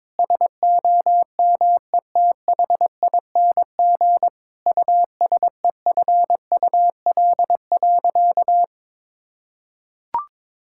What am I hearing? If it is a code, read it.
Morse code, heard as a keyed tone, about 22 wpm, 700 Hz. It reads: SOMETHING USEFUL.
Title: Sentences from Top 1000 Words - Review Part 5 - 22wpm